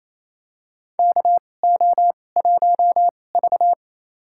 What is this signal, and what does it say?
Morse code, heard as a keyed tone, about 27 wpm, spaced as for 20 wpm, 700 Hz. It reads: KO1V
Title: USA Call Signs - 4 Characters - 2x Character Spacing - 28wpm